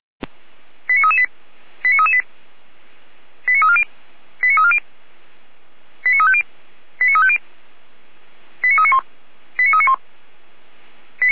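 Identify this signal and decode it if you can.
Five-tone selective call (ZVEI), heard as quick runs of five tones, tones of about 70 ms. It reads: 89338 89338 89340 89340 89350 89350 89391 89391
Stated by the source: ZVEI five-tone selective call received on an SDR.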